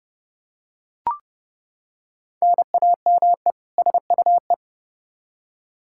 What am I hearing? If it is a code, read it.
Morse code, heard as a keyed tone, about 30 wpm, 700 Hz. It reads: NAME SUE
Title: Practice QSO - Exchange 1 - Beginning - 30wpm